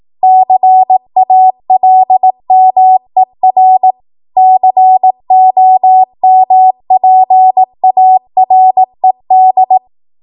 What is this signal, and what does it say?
Morse code, heard as a keyed tone, about 18 wpm, 750 Hz. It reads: CALMER COMPARED
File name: Morse / 18WPM